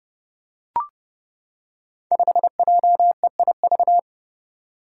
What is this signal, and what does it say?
Morse code, heard as a keyed tone, about 30 wpm, 700 Hz. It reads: 5JEIV
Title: Five Challenging Letter-Number Combinations 30wpm